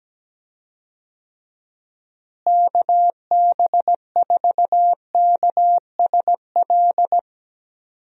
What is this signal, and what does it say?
Morse code, heard as a keyed tone, about 17 wpm, 700 Hz. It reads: KB4KSL
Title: USA Call Signs - 6 Characters - Repeated 3x - 17wpm